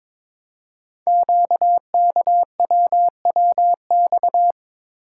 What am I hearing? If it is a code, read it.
Morse code, heard as a keyed tone, about 23 wpm, 700 Hz. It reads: QKWWX
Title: Code Groups - 5 Letters - 22wpm